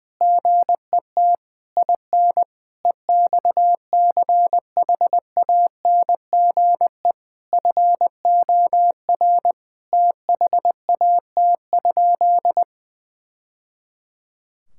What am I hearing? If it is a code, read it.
Morse code, heard as a keyed tone, about 21 wpm, 700 Hz. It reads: GET IN EXCHANGE FOR THAT?
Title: Sentences from Top 1000 Words - Review Part 6 - 20wpm